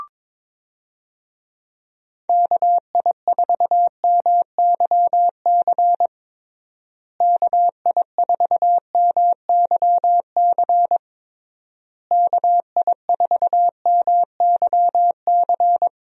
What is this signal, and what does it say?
Morse code, heard as a keyed tone, about 22 wpm, 700 Hz. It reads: KI4MYC KI4MYC KI4MYC
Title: USA Call Signs - 6 Characters - Repeated 3x - 22wpm